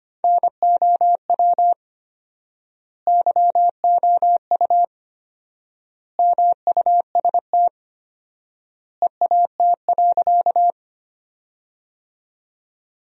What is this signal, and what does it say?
Morse code, heard as a keyed tone, about 25 wpm, 700 Hz. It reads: NOW YOU MUST EAT.
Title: 4x Word-Spacing - Sentences from Top 300 Words - 25wpm